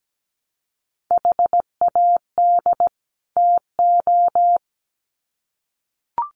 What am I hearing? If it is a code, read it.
Morse code, heard as a keyed tone, about 17 wpm, 700 Hz. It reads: HAD TO
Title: Sets of 2 Words 17wpm